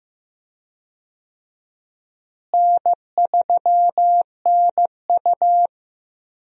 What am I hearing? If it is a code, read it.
Morse code, heard as a keyed tone, about 16 wpm, 700 Hz. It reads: N3NU
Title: USA Call Signs - 4 Characters - Repeated 3x - 15wpm